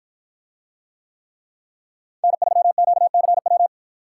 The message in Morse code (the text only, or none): N3QCW